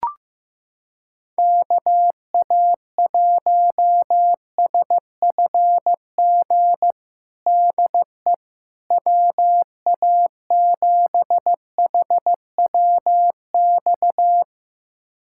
Morse code, heard as a keyed tone, about 15 wpm, 700 Hz. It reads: KA1SFG DE WA7HWX
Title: USA Callsign DE Callsign 15wpm